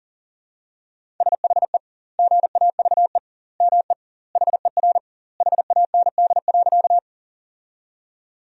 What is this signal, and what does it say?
Morse code, heard as a keyed tone, about 40 wpm, 700 Hz. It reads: SHE GAVE ME HER HAND.
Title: 2x Word-Spacing - Sentences from Top 500 Words - 40wpm